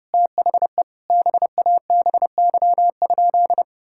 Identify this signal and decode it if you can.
Morse code, heard as a keyed tone, about 30 wpm, 700 Hz. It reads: THE BABY?
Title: Sentences from Top 1600 Words 30wpm